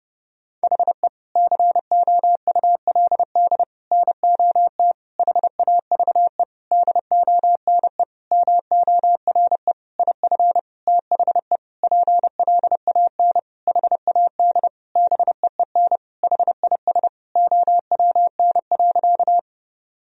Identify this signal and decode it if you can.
Morse code, heard as a keyed tone, about 30 wpm, 700 Hz. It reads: HE COULD NOT HAVE DONE MORE IF THE PLAN HAD BEEN HIS OWN.
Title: Sentences from Top 1000 Words - Review Part 2 - 30wpm